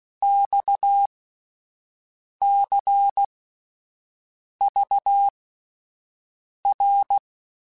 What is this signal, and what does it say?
Morse code, heard as a keyed tone, about 16 wpm, spaced as for 5 wpm, 800 Hz. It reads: XCVR